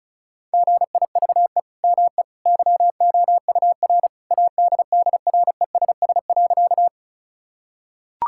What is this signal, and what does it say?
Morse code, heard as a keyed tone, about 36 wpm, 700 Hz. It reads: GIVE ME YOUR ADDRESS.
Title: Sentences from Top 1000 Words - Review Part 6 - 35wpm